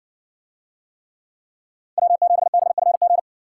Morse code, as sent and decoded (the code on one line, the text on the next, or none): .-- --... -... ..-. --.
W7BFG